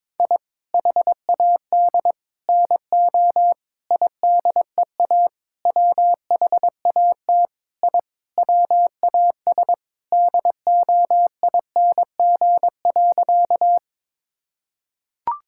Morse code, heard as a keyed tone, about 21 wpm, 700 Hz. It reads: I HAD NO IDEA WHAT I WAS DOING.
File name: practice / 22wpm / Sentences with 8-Word Phrases - 22wpm